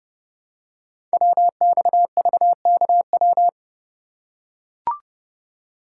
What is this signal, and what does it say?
Morse code, heard as a keyed tone, about 30 wpm, 700 Hz. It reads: WXVKW
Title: Code Groups - 5 Letters - 30wpm